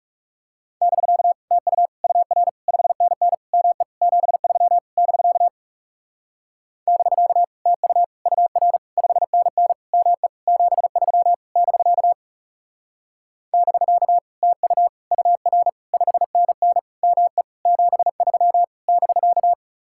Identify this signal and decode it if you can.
Morse code, heard as a keyed tone, about 37 wpm, 700 Hz. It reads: <BK> TU UR 5NN ME 73 <BK> <BK> TU UR 5NN ME 73 <BK> <BK> TU UR 5NN ME 73 <BK>
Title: Speed-Racing Common QSO Phrases - 30wpm